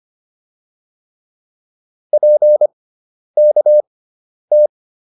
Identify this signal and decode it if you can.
Morse code, heard as a keyed tone, about 25 wpm, spaced as for 10 wpm, 600 Hz. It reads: PKT